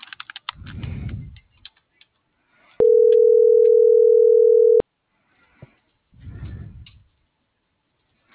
Ringback tone (North America), 440 + 480 Hz.